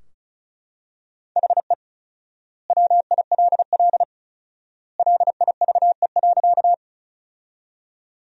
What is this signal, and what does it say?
Morse code, heard as a keyed tone, about 34 wpm, 700 Hz. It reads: HE WILL LIVE.
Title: 4x Word-Spacing - Sentences from Top 200 Words - 35wpm